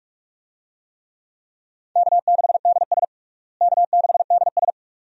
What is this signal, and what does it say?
Morse code, heard as a keyed tone, about 46 wpm, 700 Hz. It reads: K6DS K6DS
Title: USA Call Signs - 4 Characters - Repeated 2x - 45wpm